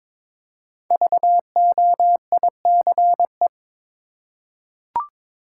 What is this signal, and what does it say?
Morse code, heard as a keyed tone, about 22 wpm, 700 Hz. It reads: VOICE
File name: 5 Letter Words - First 3 Letters Given - 22wpm